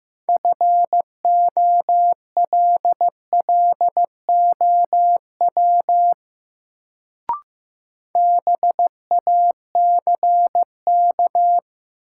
Morse code, heard as a keyed tone, about 16 wpm, 700 Hz. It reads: FOLLOW BACK
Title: Sentences from Top 200 Words - New Word Drill - 15wpm